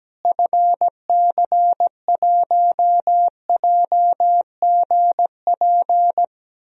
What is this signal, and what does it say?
Morse code, heard as a keyed tone, about 17 wpm, 700 Hz. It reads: FC1JGP
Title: DXCC-100 Call Signs - 6 Characters - Repeated 3x - 17wpm